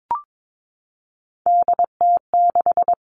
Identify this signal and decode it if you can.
Morse code, heard as a keyed tone, about 22 wpm, 700 Hz. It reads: DT6
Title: All Cut Numbers - 3-Digit 22wpm